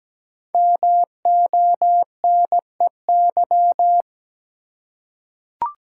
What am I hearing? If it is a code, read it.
Morse code, heard as a keyed tone, about 17 wpm, 700 Hz. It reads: MONEY
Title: Five Letter Words - 17wpm